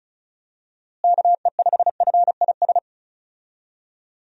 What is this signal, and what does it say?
Morse code, heard as a keyed tone, about 36 wpm, 700 Hz. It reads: KE5FIS
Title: USA Call Signs - 6 Characters - Repeated 2x - 35wpm